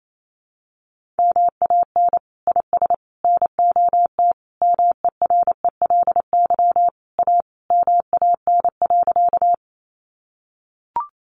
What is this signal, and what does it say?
Morse code, heard as a keyed tone, about 28 wpm, 700 Hz. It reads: MAN IS NOT MERELY A MAN.
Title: Sentences from Top 1400 Words 28wpm